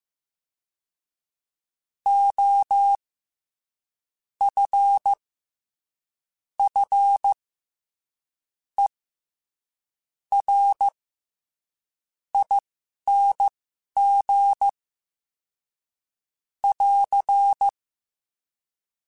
Morse code, heard as a keyed tone, about 15 wpm, spaced as for 6 wpm, 800 Hz. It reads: OFFERING <AR>
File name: Special INGs-mp3s / Special INGs-mp3s / SlowToFaster ing @10